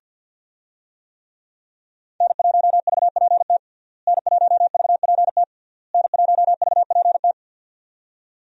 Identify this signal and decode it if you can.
Morse code, heard as a keyed tone, about 47 wpm, 700 Hz. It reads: N1VPT N1VPT N1VPT
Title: USA Call Signs - 5 Characters - Repeated 3x - 50wpm